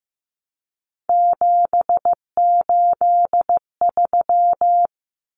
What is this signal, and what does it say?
Morse code, heard as a keyed tone, about 15 wpm, 700 Hz. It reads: 783
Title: Common Cut Numbers - 3-Digit 15wpm